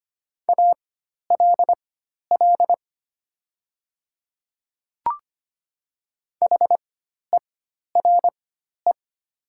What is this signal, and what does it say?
Morse code, heard as a keyed tone, about 25 wpm, spaced as for 12 wpm, 700 Hz. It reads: ALL HERE
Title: 4x Character Spacing - Top 200 Words - 25wpm